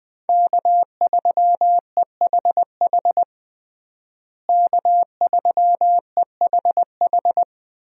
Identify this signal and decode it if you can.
Morse code, heard as a keyed tone, about 20 wpm, 700 Hz. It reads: K3EHH K3EHH
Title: USA Call Signs - 5 Characters - Repeated 3x - 20wpm